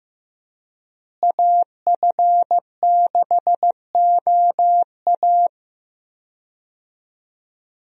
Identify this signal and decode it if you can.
Morse code, heard as a keyed tone, about 16 wpm, 700 Hz. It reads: AF6OA
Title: USA Call Signs - 5 Characters - Repeated 2x - 15wpm